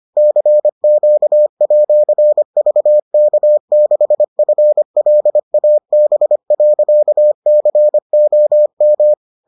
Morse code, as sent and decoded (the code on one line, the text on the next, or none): -.-. --.- .--.-. ...- -.- -.... ..-. .-.. .- -... .-.-.- -.-. --- --
CQ@VK6FLAB.COM